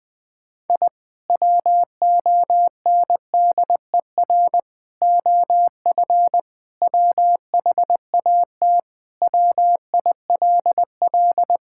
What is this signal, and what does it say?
Morse code, heard as a keyed tone, about 20 wpm, 700 Hz. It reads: I WONDER OF WHAT WILL
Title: Sentences with 2-Word Phrases - Spoken Beginning - 20wpm